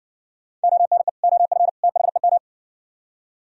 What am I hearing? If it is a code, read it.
Morse code, heard as a keyed tone, about 59 wpm, 700 Hz. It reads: ONE OF THEM